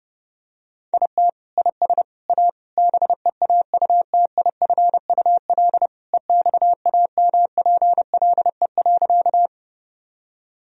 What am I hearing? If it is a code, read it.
Morse code, heard as a keyed tone, about 32 wpm, 700 Hz. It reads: IT IS A BEAUTIFUL EXAMPLE.
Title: Sentences from Top 1000 Words - Review Part 1 - 30wpm